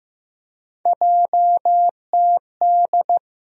Morse code, heard as a keyed tone, about 15 wpm, 700 Hz. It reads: JTD
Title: USA Call Sign Postfixes - 3 Letters 15wpm